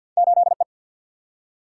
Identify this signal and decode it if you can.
Morse code, heard as a keyed tone, about 50 wpm, 700 Hz. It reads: 8 E
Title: Single-Letter Single-Number 50wpm